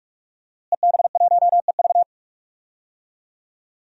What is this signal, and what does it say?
Morse code, heard as a keyed tone, about 45 wpm, 700 Hz. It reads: EB1EV
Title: DXCC-100 Call Signs - 5 Characters - Repeated 2x - 45wpm